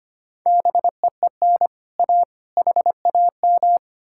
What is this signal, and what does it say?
Morse code, heard as a keyed tone, about 33 wpm, 700 Hz. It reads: BEEN A HAM